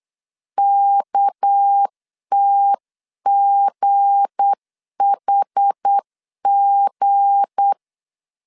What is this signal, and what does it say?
Morse code, heard as a keyed tone, about 8 wpm, 800 Hz. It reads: KTGHG
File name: LEZIONI / 7 GF9Z / 7 G 8wpm 3118